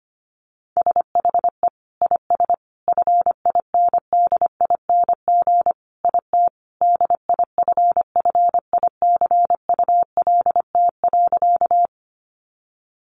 Morse code, heard as a keyed tone, about 25 wpm, 700 Hz. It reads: SHE IS FINDING IT DIFFICULT.
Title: Sentences from Top 1100 Words 25wpm